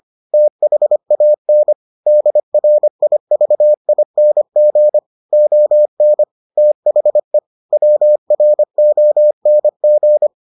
Morse code, heard as a keyed tone, about 25 wpm, 600 Hz. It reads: THAN DRIVING ON THE WRONG